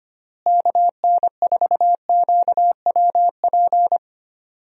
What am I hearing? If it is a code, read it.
Morse code, heard as a keyed tone, about 25 wpm, 700 Hz. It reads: KN4QWP
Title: USA Call Signs - 6 Characters - Repeated 3x - 25wpm